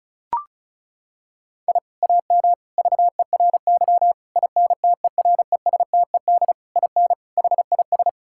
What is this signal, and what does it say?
Morse code, heard as a keyed tone, about 34 wpm, 700 Hz. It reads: I AM VERY INTERESTED IN HIS
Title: Sentences from Top 1000 Words - Review Part 6 - 35wpm